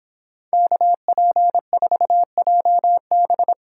Morse code, heard as a keyed tone, about 26 wpm, 700 Hz. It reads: KP4JB